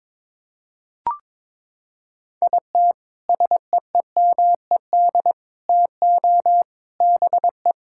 Morse code, heard as a keyed tone, about 22 wpm, 700 Hz. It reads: IT SEEMED TO BE
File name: practice / 22wpm / Sentences with 3-Word Phrases - 22wpm